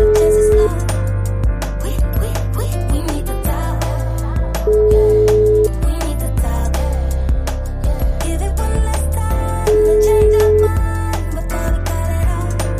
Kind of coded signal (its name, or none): telephone call progress tone